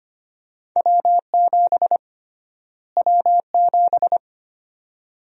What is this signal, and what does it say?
Morse code, heard as a keyed tone, about 25 wpm, 700 Hz. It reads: W7 W7
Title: USA Call Sign Prefixes - Repeated 2x - 25wpm